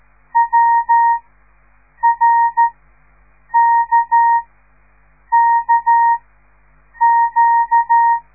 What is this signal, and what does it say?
Morse code, heard as a keyed tone, about 13 wpm, spaced as for 7 wpm, 930 Hz. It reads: WRKKQ
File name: radiomorningstar / morse / les 5